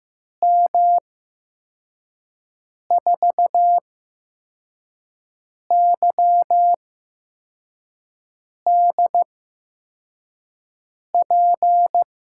Morse code, heard as a keyed tone, about 15 wpm, spaced as for 4 wpm, 700 Hz. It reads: M4YDP
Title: USA Call Signs - 6 Characters - 8x Character Spacing - 15wpm